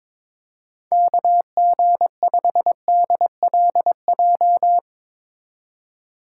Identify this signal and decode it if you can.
Morse code, heard as a keyed tone, about 21 wpm, 700 Hz. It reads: KG5DLJ
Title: USA Call Signs - 6x Character Spacing - 22wpm